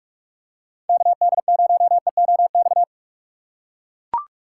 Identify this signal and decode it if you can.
Morse code, heard as a keyed tone, about 44 wpm, 700 Hz. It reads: KD0EOX